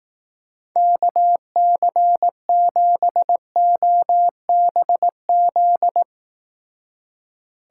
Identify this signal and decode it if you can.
Morse code, heard as a keyed tone, about 19 wpm, 700 Hz. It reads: KC7OBZ
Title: Speed Racing - USA 2x3 Call Signs 15wpm